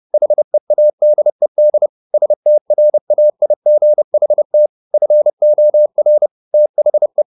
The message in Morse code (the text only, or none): HEADED STRAIGHT FOR THE